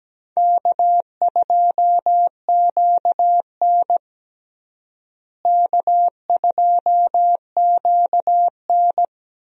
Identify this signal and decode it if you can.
Morse code, heard as a keyed tone, about 17 wpm, 700 Hz. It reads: K2QN K2QN
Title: USA Call Signs - 4 Characters - Repeated 2x - 17wpm